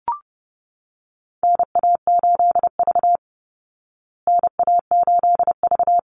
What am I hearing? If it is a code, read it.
Morse code, heard as a keyed tone, about 30 wpm, 700 Hz. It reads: NA8V NA8V
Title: Call Signs Repeated 3x - 30wpm